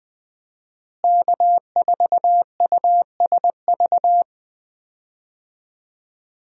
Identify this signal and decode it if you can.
Morse code, heard as a keyed tone, about 20 wpm, 700 Hz. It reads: K4USV
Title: USA Call Signs - 5 Characters - Repeated 2x - 20wpm